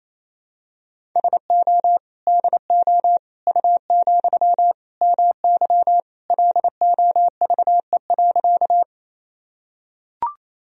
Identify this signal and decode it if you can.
Morse code, heard as a keyed tone, about 28 wpm, 700 Hz. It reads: SO DO U, MY LOVE.
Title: Sentences from Top 400 Words and Abbreviations 28wpm